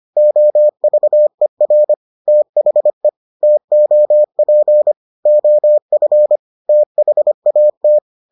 Morse code, heard as a keyed tone, about 25 wpm, 600 Hz. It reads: OVER THE TOP OF THAT